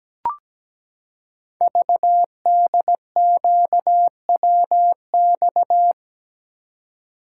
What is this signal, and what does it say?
Morse code, heard as a keyed tone, about 17 wpm, 700 Hz. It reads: VDQWX
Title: Code Groups - 5 Letter:Numbers - 17wpm